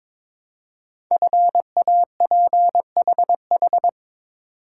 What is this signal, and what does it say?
Morse code, heard as a keyed tone, about 23 wpm, 700 Hz. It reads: FAPHH
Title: Code Groups - 5 Letters - 22wpm